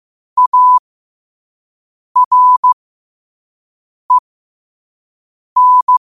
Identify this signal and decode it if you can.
Morse code, heard as a keyed tone, about 15 wpm, spaced as for 5 wpm, 1000 Hz. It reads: AREN
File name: qotdfiles / 5wpm